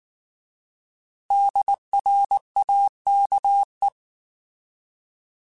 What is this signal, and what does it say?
Morse code, heard as a keyed tone, about 19 wpm, 800 Hz. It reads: DRAKE